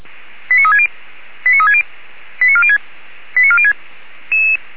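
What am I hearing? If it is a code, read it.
Five-tone selective call (ZVEI), heard as quick runs of five tones, tones of about 70 ms. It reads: 89360 89360 89406 89406
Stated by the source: ZVEI five-tone selective call received on an SDR.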